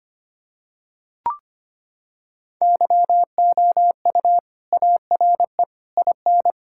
Morse code, heard as a keyed tone, about 25 wpm, 700 Hz. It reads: YOU ARE IN
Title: Sentences from Top 1600 Words 25wpm